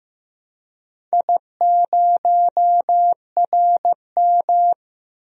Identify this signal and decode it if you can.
Morse code, heard as a keyed tone, about 15 wpm, 700 Hz. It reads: I0RM
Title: DXCC-100 Call Signs - 4 Characters - Repeated 2x - 15wpm